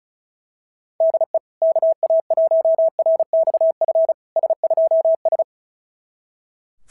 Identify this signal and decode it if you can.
Morse code, heard as a keyed tone, about 36 wpm, 690 Hz. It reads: DE KA1RXF S2S
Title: SOTA QSO Practice 35wpm